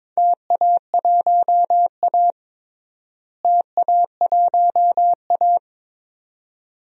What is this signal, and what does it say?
Morse code, heard as a keyed tone, about 22 wpm, 700 Hz. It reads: TA1A TA1A
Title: DXCC-100 Call Signs - 4 Characters - Repeated 2x - 22wpm